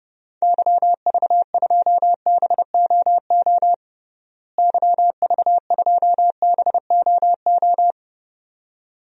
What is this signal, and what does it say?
Morse code, heard as a keyed tone, about 30 wpm, 700 Hz. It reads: YV2BOO YV2BOO
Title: DXCC-100 Call Signs - 6 Characters - Repeated 2x - 30wpm